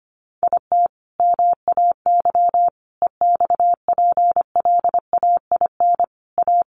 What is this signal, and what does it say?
Morse code, heard as a keyed tone, about 25 wpm, 700 Hz. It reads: IT MAY EXPLAIN A
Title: Sentences from Top 1100 Words 25wpm